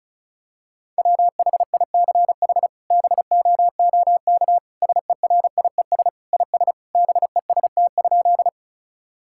Morse code, heard as a keyed tone, about 36 wpm, 700 Hz. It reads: WHICH BOOK SERIES IS BEST?
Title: Sentences from Top 1000 Words - Review Part 5 - 35wpm